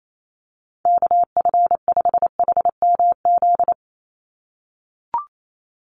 Morse code, heard as a keyed tone, about 28 wpm, 700 Hz. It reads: KF5HMZ
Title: USA Call Signs 28wpm